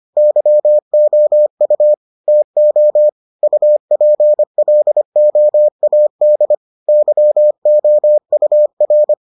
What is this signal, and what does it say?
Morse code, heard as a keyed tone, about 25 wpm, 600 Hz. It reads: YOU TO UPLOAD YOUR